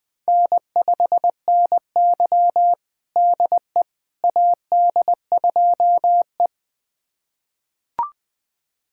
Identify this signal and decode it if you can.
Morse code, heard as a keyed tone, about 21 wpm, 700 Hz. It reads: N5NY DE AD2E
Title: USA Callsign DE Callsign - 4 Characters - 20wpm